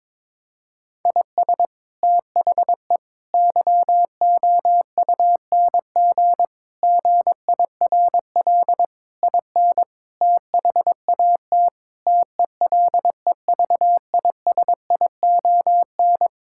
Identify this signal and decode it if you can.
Morse code, heard as a keyed tone, about 21 wpm, 700 Hz. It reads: IS THE YOUNG GIRL IN THAT TELEVISION
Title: Sentences from Top 1000 Words - Review Part 5 - 22wpm